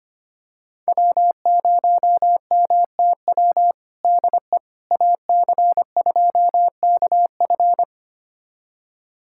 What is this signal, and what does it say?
Morse code, heard as a keyed tone, about 25 wpm, 700 Hz. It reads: W0MTW DE AC2KF
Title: USA Callsign DE Callsign - 5 Characters - 25wpm